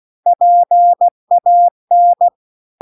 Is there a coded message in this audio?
Yes: Morse code from 0.3 to 2.3 s, heard as a keyed tone, about 16 wpm, 700 Hz. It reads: PAN